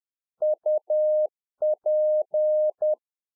Morse code, heard as a keyed tone, about 10 wpm, 610 Hz. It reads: UP
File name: 2011-04-mp3 / A SMALL TRANCEIVER 1 10WPM